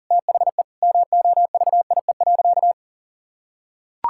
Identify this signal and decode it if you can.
Morse code, heard as a keyed tone, about 40 wpm, 700 Hz. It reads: THE MOVIE.
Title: Sentences from Top 1000 Words - Review Part 5 - 40wpm